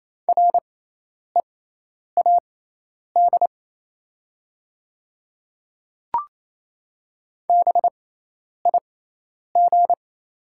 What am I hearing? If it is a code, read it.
Morse code, heard as a keyed tone, about 28 wpm, spaced as for 10 wpm, 700 Hz. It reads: READ BIG
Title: 6x Character Spacing - Top 200 Words - 28wpm